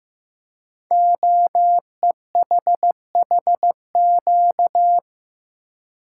Morse code, heard as a keyed tone, about 15 wpm, 700 Hz. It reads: OEHHQ
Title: Code Groups - 5 Letters - 15wpm